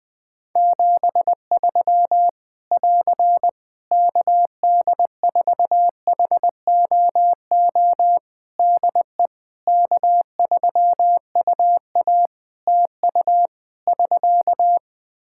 Morse code, heard as a keyed tone, about 20 wpm, 700 Hz. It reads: 73 <AR> KD4HOO DE K3UA TU <SK>
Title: Practice QSO - Exchange 4 - Ending - 20wpm